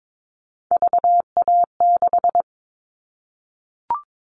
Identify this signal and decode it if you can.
Morse code, heard as a keyed tone, about 22 wpm, 700 Hz. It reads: VA6
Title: Callsign Prefixes for Top 25 DXCC - 22wpm